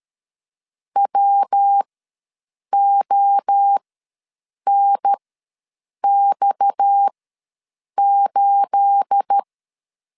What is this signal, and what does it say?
Morse code, heard as a keyed tone, about 13 wpm, spaced as for 7 wpm, 800 Hz. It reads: WONX8
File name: LEZIONI / 10 SIGNS / 10A SLASH ws 12wpm 3443